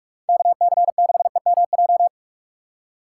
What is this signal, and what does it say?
Morse code, heard as a keyed tone, about 44 wpm, 700 Hz. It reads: KC6EMJ